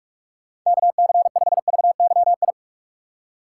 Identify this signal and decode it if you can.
Morse code, heard as a keyed tone, about 45 wpm, 700 Hz. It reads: KC5VYI
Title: USA Call Signs - 6 Characters - 6x Character Spacing - 45wpm